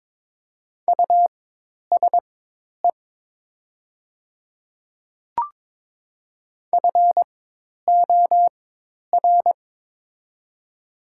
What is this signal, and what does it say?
Morse code, heard as a keyed tone, about 22 wpm, spaced as for 10 wpm, 700 Hz. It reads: USE FOR
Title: 4x Character Spacing - Top 100 Words - 22wpm